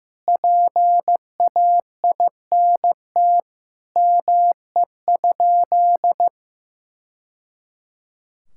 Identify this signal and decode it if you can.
Morse code, heard as a keyed tone, about 15 wpm, 700 Hz. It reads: PAINT ME?
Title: Sentences from Top 1000 Words - Review Part 6 - 15wpm